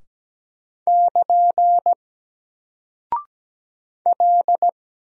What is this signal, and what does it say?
Morse code, heard as a keyed tone, about 17 wpm, 700 Hz. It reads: <KN> L
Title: Single Character (Letter:Number:Punctuation:Prosign) Alt 17wpm